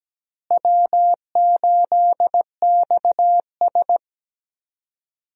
Morse code, heard as a keyed tone, about 17 wpm, 700 Hz. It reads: W8XS
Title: USA Call Signs - 4 Characters - Repeated 3x - 17wpm